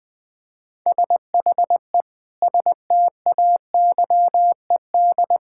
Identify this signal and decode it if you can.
Morse code, heard as a keyed tone, about 20 wpm, 700 Hz. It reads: SHE STAYED
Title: Sentences with 5-Word Phrases - Spoken Beginning - 20wpm